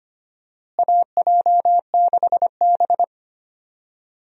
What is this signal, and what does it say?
Morse code, heard as a keyed tone, about 25 wpm, 700 Hz. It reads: AJ6B
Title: USA Call Signs - 4 Characters - Repeated 3x - 25wpm